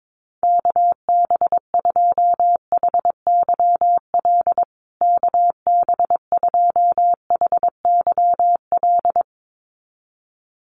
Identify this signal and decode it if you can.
Morse code, heard as a keyed tone, about 22 wpm, 700 Hz. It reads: KB2HYL KB2HYL
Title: USA Call Signs Repeated 2x 22wpm